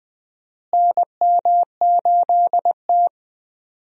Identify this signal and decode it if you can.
Morse code, heard as a keyed tone, about 20 wpm, 700 Hz. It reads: NM8T
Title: USA Call Signs - 4 Characters - Repeated 3x - 20wpm